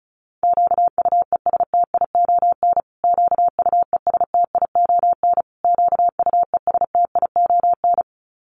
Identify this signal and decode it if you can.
Morse code, heard as a keyed tone, about 34 wpm, 700 Hz. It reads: QUESTION QUESTION QUESTION
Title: Top 500 Words Repeated 35wpm